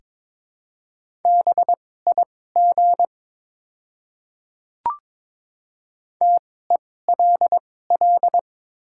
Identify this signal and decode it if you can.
Morse code, heard as a keyed tone, about 22 wpm, spaced as for 16 wpm, 700 Hz. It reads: BIG TELL
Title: 2x Character Spacing - Top 200 Words - 22wpm